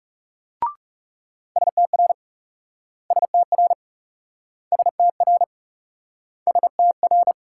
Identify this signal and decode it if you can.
Morse code, heard as a keyed tone, about 38 wpm, 700 Hz. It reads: STR STR STR STR